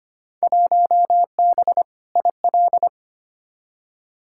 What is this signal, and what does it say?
Morse code, heard as a keyed tone, about 25 wpm, 700 Hz. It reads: 1B IL
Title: Field Day Exchanges - 25wpm